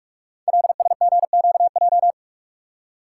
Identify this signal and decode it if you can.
Morse code, heard as a keyed tone, about 46 wpm, 700 Hz. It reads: LSGQJ
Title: Code Groups - 5 Letter:Numbers - 45wpm